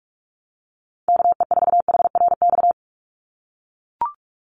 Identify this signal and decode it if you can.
Morse code, heard as a keyed tone, about 44 wpm, 700 Hz. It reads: KE4HRX
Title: USA Call Signs 45wpm